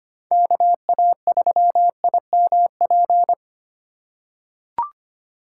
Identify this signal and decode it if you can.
Morse code, heard as a keyed tone, about 25 wpm, 700 Hz. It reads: KA3IMP
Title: USA Call Signs - 6x Character Spacing - 25wpm